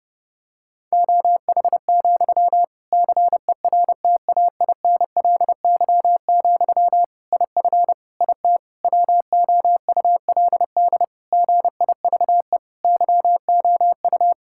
Morse code, heard as a keyed tone, about 31 wpm, 700 Hz. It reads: OH, CERTAINLY, IF IT WOULD GIVE YOU
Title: Sentences from Top 1600 Words 30wpm